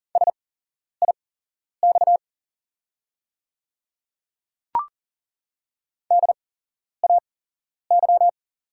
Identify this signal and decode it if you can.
Morse code, heard as a keyed tone, about 40 wpm, spaced as for 11 wpm, 700 Hz. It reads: SIX DAY
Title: 8x Character Spacing - Top 500 Words - 40wpm